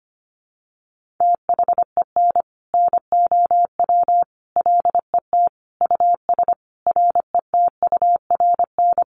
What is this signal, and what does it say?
Morse code, heard as a keyed tone, about 25 wpm, 700 Hz. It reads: THEN NOW LET US RETURN
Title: Sentences from Top 1500 Words 25wpm